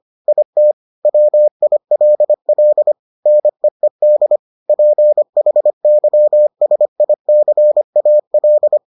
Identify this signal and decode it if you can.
Morse code, heard as a keyed tone, about 25 wpm, 600 Hz. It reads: IT WILL NEED PHYSICAL